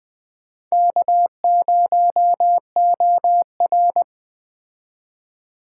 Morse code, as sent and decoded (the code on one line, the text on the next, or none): -.- ----- --- .-.
K0OR